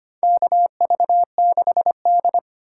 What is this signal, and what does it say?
Morse code, heard as a keyed tone, about 25 wpm, 700 Hz. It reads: KV6D
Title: USA Call Signs - 2x Character Spacing - 25wpm